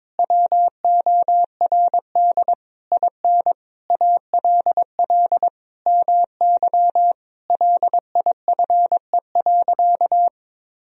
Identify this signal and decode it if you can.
Morse code, heard as a keyed tone, about 22 wpm, 700 Hz. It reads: WORD IN ALL MY LIFE.
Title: Sentences from Top 1000 Words - Review Part 1 - 22wpm